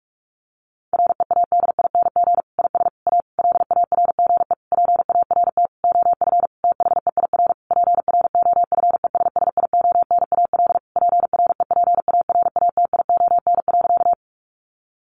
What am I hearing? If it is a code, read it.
Morse code, heard as a keyed tone, about 44 wpm, 700 Hz. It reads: READING IS A LARGE PART OF THEIR PROFESSIONAL PREPARATION.
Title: Sentences from Top 1400 Words 45wpm